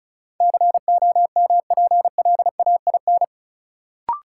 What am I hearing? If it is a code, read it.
Morse code, heard as a keyed tone, about 36 wpm, 700 Hz. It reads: COMPLAIN
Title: Sentences from Top 1300 Words - New Word Drill Alt - 35wpm